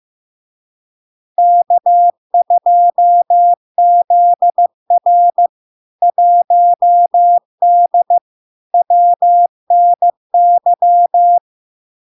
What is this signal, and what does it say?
Morse code, heard as a keyed tone, about 15 wpm, 700 Hz. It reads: K2ZR 1D WNY
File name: LARK-FD-2022-Log-in-CW-15wpm-700-Hz